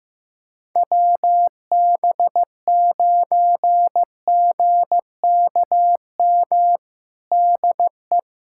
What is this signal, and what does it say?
Morse code, heard as a keyed tone, about 15 wpm, 700 Hz. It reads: WB9GKM DE
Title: USA Callsign DE Callsign - 6 Characters - 15wpm